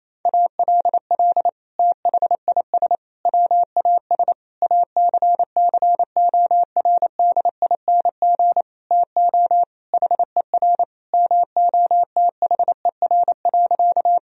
Morse code, heard as a keyed tone, about 28 wpm, 700 Hz. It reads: ALL THIS WAS ACCORDING TO HER MOTHER.
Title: Sentences from Top 1000 Words - Review Part 4 - 28wpm